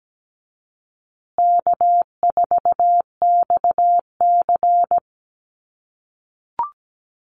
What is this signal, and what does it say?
Morse code, heard as a keyed tone, about 17 wpm, 700 Hz. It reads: K4XC
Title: USA Call Signs Repeated 3x 17wpm